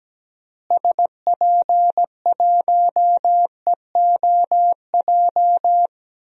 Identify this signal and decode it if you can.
Morse code, heard as a keyed tone, about 17 wpm, 700 Hz. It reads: SP1EOJ
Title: DXCC-100 Call Signs - 6 Characters - Repeated 3x - 17wpm